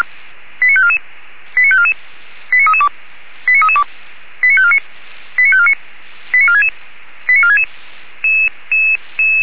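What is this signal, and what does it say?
Five-tone selective call (ZVEI), heard as quick runs of five tones, tones of about 70 ms. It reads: 89544 89544 89332 89332 89549 89549 89466 89466
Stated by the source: ZVEI five-tone call received by SDR, server of feuerwehr-orlamuende.org (fire brigade Orlamünde).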